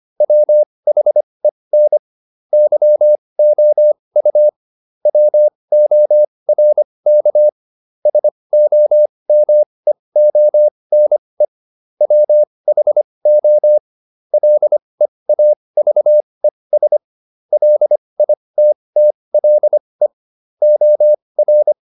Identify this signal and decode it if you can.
Morse code, heard as a keyed tone, about 25 wpm, spaced as for 20 wpm, 600 Hz. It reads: WHEN YOU WORK SOMEONE WHO LEAVES LITTLE OR